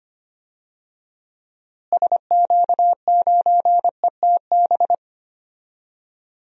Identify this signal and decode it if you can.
Morse code, heard as a keyed tone, about 25 wpm, 700 Hz. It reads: SQ9ETB
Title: DXCC-100 Call Signs - 6 Characters - Repeated 2x - 25wpm